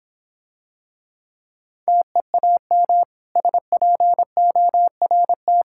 Morse code, heard as a keyed tone, about 25 wpm, 700 Hz. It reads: TEAM SPORT
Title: Difficult Words - Speed-Racing with Hints - 20wpm